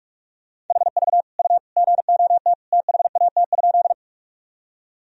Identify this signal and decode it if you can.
Morse code, heard as a keyed tone, about 43 wpm, 700 Hz. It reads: HV U GOT THAT?
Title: Sentences from Top 300 Words and Abbreviations 45wpm